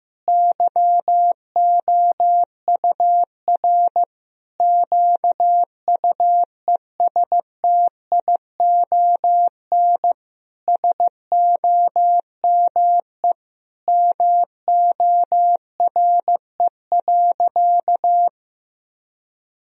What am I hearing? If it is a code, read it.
Morse code, heard as a keyed tone, about 15 wpm, 700 Hz. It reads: YOUR QUESTION SOME MORE.
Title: Sentences from Top 1000 Words - Review Part 2 - 15wpm